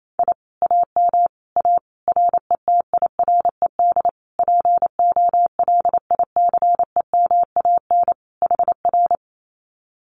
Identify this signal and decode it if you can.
Morse code, heard as a keyed tone, about 28 wpm, 700 Hz. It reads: I AM A RETIRED POLICEMAN HR
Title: Common QSO Phrases Encore 28wpm